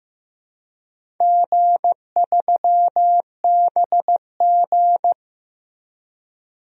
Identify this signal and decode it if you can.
Morse code, heard as a keyed tone, about 15 wpm, 700 Hz. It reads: G3BG
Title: DXCC-100 Call Signs - 4 Characters - Repeated 2x - 15wpm